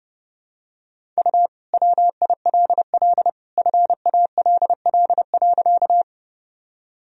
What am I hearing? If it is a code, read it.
Morse code, heard as a keyed tone, about 30 wpm, 700 Hz. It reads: U WILL FALL.
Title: Sentences from Top 500 Words and Abbreviations 30wpm